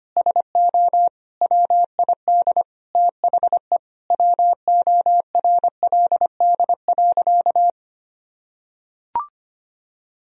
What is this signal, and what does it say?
Morse code, heard as a keyed tone, about 25 wpm, 700 Hz. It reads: SO WID THE WORLD.
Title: Sentences from Top 200 Words and Abbreviations 25wpm